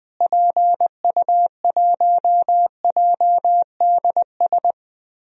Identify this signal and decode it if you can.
Morse code, heard as a keyed tone, about 20 wpm, 700 Hz. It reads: PU1JDS
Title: DXCC-100 Call Signs - 6 Characters - Repeated 2x - 20wpm